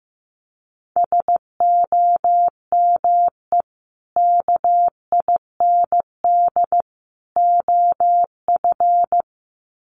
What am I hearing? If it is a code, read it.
Morse code, heard as a keyed tone, about 15 wpm, 700 Hz. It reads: SOME KIND OF
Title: Sets of 3 Words 15wpm